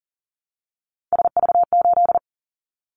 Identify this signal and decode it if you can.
Morse code, heard as a keyed tone, about 42 wpm, 700 Hz. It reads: SV8
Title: Callsign Prefixes for Top 50 DXCC - 40wpm